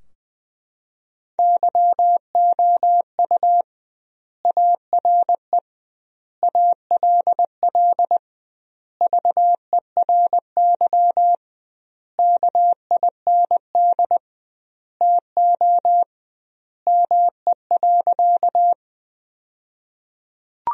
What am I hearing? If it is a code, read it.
Morse code, heard as a keyed tone, about 20 wpm, 700 Hz. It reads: YOU ARE ALL VERY KIND TO ME.
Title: 2x Word-Spacing - Sentences from Top 200 Words - 20wpm